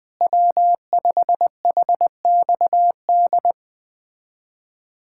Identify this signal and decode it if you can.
Morse code, heard as a keyed tone, about 21 wpm, 700 Hz. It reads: W5HXD
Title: USA Callsign DE Callsign 20wpm